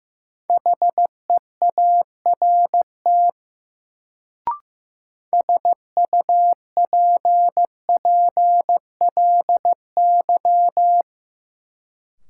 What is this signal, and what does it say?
Morse code, heard as a keyed tone, about 15 wpm, 700 Hz. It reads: HEART SUPPLY
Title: Sentences from Top 1200 Words - New Word Drill Alt - 15wpm